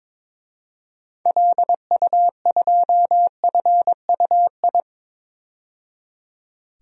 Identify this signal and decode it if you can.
Morse code, heard as a keyed tone, about 22 wpm, 700 Hz. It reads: LU2FUI
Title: DXCC-100 Call Signs - 6 Characters - 22wpm